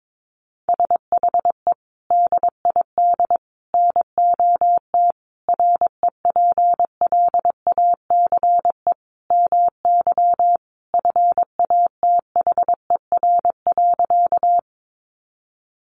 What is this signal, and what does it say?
Morse code, heard as a keyed tone, about 22 wpm, 700 Hz. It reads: SHE DID NOT REPLACE MY FATHER.
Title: Sentences from Top 1400 Words 22wpm